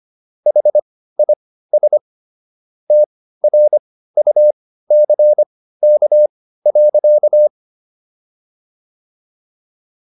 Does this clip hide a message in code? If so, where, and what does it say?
Yes: Morse code from 0.5 to 7.5 s, heard as a keyed tone, about 25 wpm, spaced as for 15 wpm, 600 Hz. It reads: HIS TRUCK.